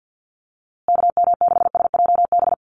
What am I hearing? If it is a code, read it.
Morse code, heard as a keyed tone, about 49 wpm, 700 Hz. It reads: KM6SJB
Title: USA Call Signs Repeated 3x 50wpm